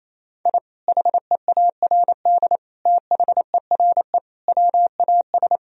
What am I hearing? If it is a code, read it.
Morse code, heard as a keyed tone, about 27 wpm, 700 Hz. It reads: I HEARD THERE WAS
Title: Sentences from Top 1000 Words - Review Part 2 - 28wpm